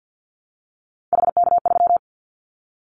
Morse code, heard as a keyed tone, about 50 wpm, 690 Hz. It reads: HK3
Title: Callsign Prefixes for Top 100 DXCC - 50wpm